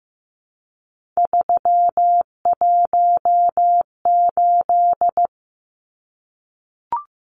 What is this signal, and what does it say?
Morse code, heard as a keyed tone, about 15 wpm, 700 Hz. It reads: 318
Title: Numbers - 3-Digit 15wpm